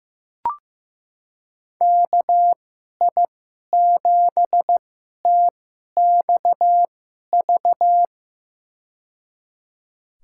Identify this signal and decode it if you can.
Morse code, heard as a keyed tone, about 15 wpm, spaced as for 11 wpm, 700 Hz. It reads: KI7TXV
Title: USA Call Signs - 6 Characters - 2x Character Spacing - 15wpm